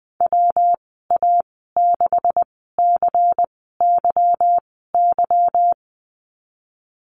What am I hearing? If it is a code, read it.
Morse code, heard as a keyed tone, about 20 wpm, spaced as for 15 wpm, 700 Hz. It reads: WA6CYY
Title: 147 Lesson 24g - Letter y - Call Signs 20wpm